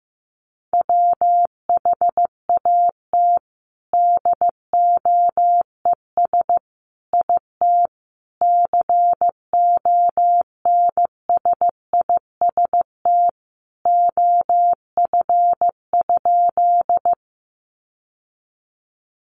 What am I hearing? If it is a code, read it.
Morse code, heard as a keyed tone, about 15 wpm, 700 Hz. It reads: WHAT DOES IT CONSIST OF?
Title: Sentences from Top 1400 Words 15wpm